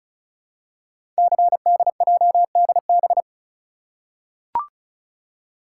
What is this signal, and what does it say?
Morse code, heard as a keyed tone, about 36 wpm, 700 Hz. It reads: CDJDB
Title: Code Groups - 5 Letters - 35wpm